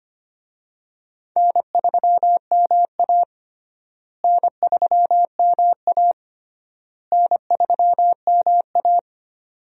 Morse code, heard as a keyed tone, about 25 wpm, 700 Hz. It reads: N3MA N3MA N3MA
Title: USA Call Signs - 4 Characters - Repeated 3x - 25wpm